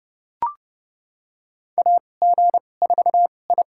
Morse code, heard as a keyed tone, about 32 wpm, spaced as for 22 wpm, 700 Hz. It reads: AG4I
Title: USA Call Signs - 4 Characters - 2x Character Spacing - 30wpm